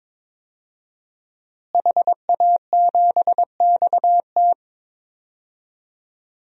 Morse code, heard as a keyed tone, about 21 wpm, 700 Hz. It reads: HA7XT
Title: DXCC-100 Call Signs - 5 Characters - Repeated 2x - 22wpm